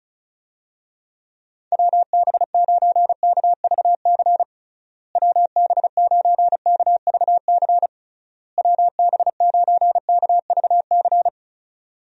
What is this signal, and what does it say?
Morse code, heard as a keyed tone, about 34 wpm, 700 Hz. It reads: WB9KVC WB9KVC WB9KVC
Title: USA Call Signs - 6 Characters - Repeated 3x - 35wpm